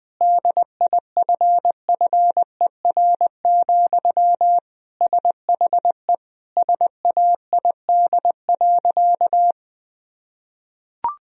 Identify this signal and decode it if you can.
Morse code, heard as a keyed tone, about 21 wpm, 700 Hz. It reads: DIFFER, SHE SAID.
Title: Sentences from Top 1000 Words - Review Part 1 - 20wpm